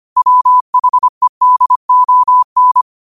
Morse code, heard as a keyed tone, about 25 wpm, 1000 Hz. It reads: WHEDON
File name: qotdfiles / 25wpm